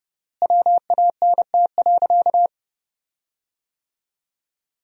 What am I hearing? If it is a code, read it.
Morse code, heard as a keyed tone, about 29 wpm, 700 Hz. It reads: WANT.
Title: 3x Word-Spacing - Sentences from Top 200 Words - 30wpm